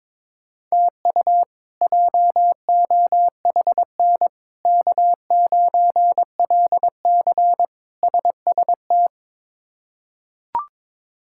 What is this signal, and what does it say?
Morse code, heard as a keyed tone, about 22 wpm, 700 Hz. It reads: TU JOHN K9LC SST
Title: K1USN SST QSO Practice - 22wpm